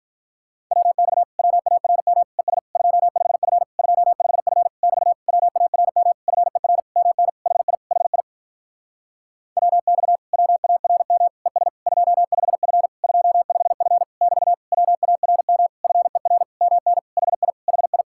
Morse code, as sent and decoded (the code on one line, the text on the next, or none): .-- -..- / .-- .- .-. -- / . ... / ..--- ..... ..-. / ..--- ..... ..-. / -...- / .-- .- .-. -- / ..-. .-. / -- -. / .... .. / .... .. / .-- -..- / .-- .- .-. -- / . ... / ..--- ..... ..-. / ..--- ..... ..-. / -...- / .-- .- .-. -- / ..-. .-. / -- -. / .... .. / .... ..
WX WARM ES 25F 25F <BT> WARM FER MN HI HI WX WARM ES 25F 25F <BT> WARM FER MN HI HI